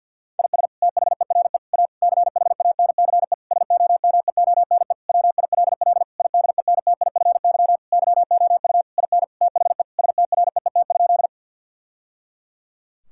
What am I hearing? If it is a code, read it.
Morse code, heard as a keyed tone, about 45 wpm, 700 Hz. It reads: IS THERE A CHANCE SOMEONE WILL IDENTIFY YOU IN THE STREET?
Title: Sentences from Top 1000 Words - Review Part 6 - 50wpm